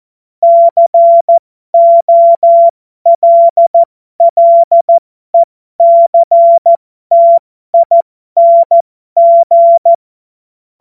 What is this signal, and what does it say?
Morse code, heard as a keyed tone, about 14 wpm, spaced as for 12 wpm, 680 Hz. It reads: COLLECTING
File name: Morsekurs / Witze